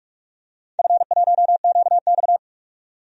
Morse code, heard as a keyed tone, about 46 wpm, 700 Hz. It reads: F1QX
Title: DXCC-100 Call Signs - 4 Characters - 45wpm